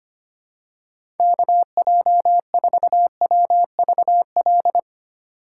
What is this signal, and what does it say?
Morse code, heard as a keyed tone, about 25 wpm, 700 Hz. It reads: KJ4WVL
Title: USA Call Signs - 2x Character Spacing - 25wpm